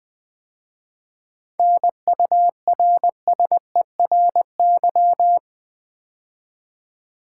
Morse code, heard as a keyed tone, about 20 wpm, 700 Hz. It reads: NURSERY
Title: 7 Letter Words - First 3 Letters Given - 20wpm